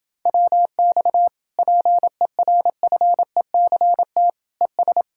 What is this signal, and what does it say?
Morse code, heard as a keyed tone, about 28 wpm, 700 Hz. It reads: WX PERFECT ES